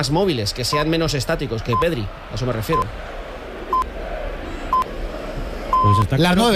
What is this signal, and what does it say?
Radio time signal (time pips), with speech. It is heard from 0.7 to 6.0 s.